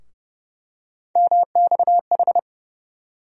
Morse code, heard as a keyed tone, about 30 wpm, 700 Hz. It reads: MXH
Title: USA Call Sign Postfixes - 3 Letters 30wpm